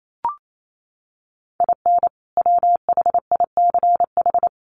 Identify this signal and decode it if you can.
Morse code, heard as a keyed tone, about 28 wpm, 700 Hz. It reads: IN WHICH
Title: Sets of 2 Words 28wpm